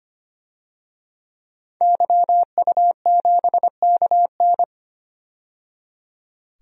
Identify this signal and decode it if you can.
Morse code, heard as a keyed tone, about 25 wpm, 700 Hz. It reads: YU7KN
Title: DXCC-100 Call Signs - 5 Characters - 25wpm